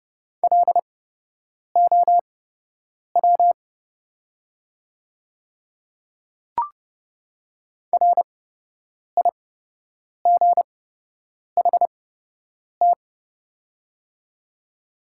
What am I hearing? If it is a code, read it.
Morse code, heard as a keyed tone, about 30 wpm, spaced as for 8 wpm, 700 Hz. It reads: LOW RIGHT
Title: 8x Character Spacing - Top 200 Words - 30wpm